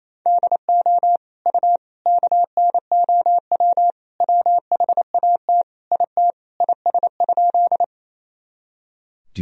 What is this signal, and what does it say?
Morse code, heard as a keyed tone, about 28 wpm, 700 Hz. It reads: DO U KNOW WHAT IT IS?
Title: Sentences from Top 100 Words and Abbreviations 28wpm